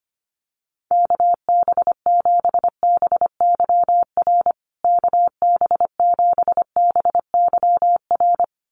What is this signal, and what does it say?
Morse code, heard as a keyed tone, about 25 wpm, 700 Hz. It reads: KB7BYR KB7BYR
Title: USA Call Signs Repeated 2x 25wpm